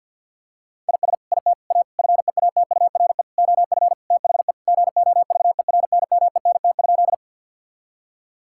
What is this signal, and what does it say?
Morse code, heard as a keyed tone, about 50 wpm, 700 Hz. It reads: IS IT A FEATURE OF THE GOVERNMENT?
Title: Sentences from Top 1000 Words - Review Part 5 - 50wpm